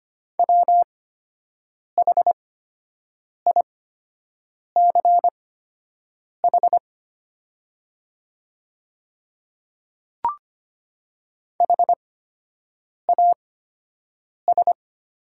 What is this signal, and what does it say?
Morse code, heard as a keyed tone, about 25 wpm, spaced as for 7 wpm, 700 Hz. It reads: WHICH HAS
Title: 8x Character Spacing - Top 100 Words - 25wpm